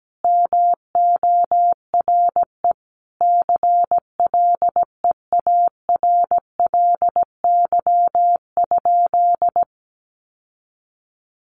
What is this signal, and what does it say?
Morse code, heard as a keyed tone, about 17 wpm, 700 Hz. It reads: MORE CLEARLY?
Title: Sentences from Top 1300 Words 17wpm